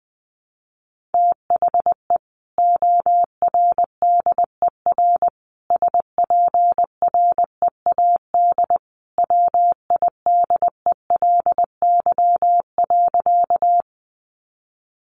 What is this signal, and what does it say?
Morse code, heard as a keyed tone, about 19 wpm, 700 Hz. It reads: THE ORDER SPREAD WIDELY.
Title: Sentences from Top 1200 Words 20wpm